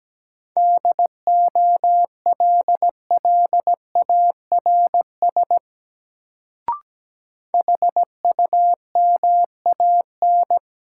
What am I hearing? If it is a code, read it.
Morse code, heard as a keyed tone, about 17 wpm, 700 Hz. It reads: DOLLARS HUMAN
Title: Sets of 5 Words - New Word Drill - 17wpm